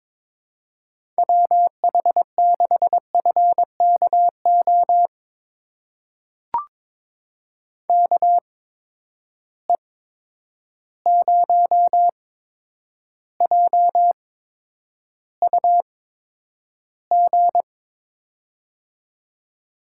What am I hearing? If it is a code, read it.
Morse code, heard as a keyed tone, about 21 wpm, spaced as for 9 wpm, 700 Hz. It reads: WH6FKO KE0JUG
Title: USA Call Signs - 6 Characters - 8x Character Spacing - 22wpm